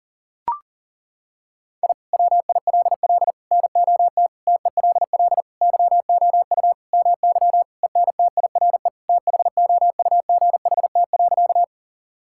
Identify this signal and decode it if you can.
Morse code, heard as a keyed tone, about 37 wpm, 700 Hz. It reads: I WILL NOT TELL YOU MY ENTIRE THOUGHT.
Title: Sentences from Top 1000 Words - Review Part 5 - 40wpm